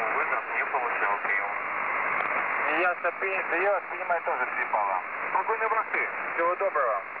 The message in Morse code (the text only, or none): none